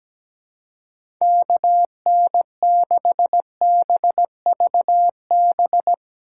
Morse code, heard as a keyed tone, about 17 wpm, 700 Hz. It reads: KN6BVB
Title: USA Call Signs - 6 Characters - 4x Character Spacing - 17wpm